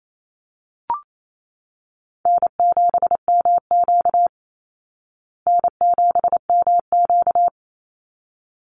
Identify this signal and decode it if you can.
Morse code, heard as a keyed tone, about 28 wpm, 690 Hz. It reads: N7MQ N7MQ
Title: Call Signs Repeated 3x - 28wpm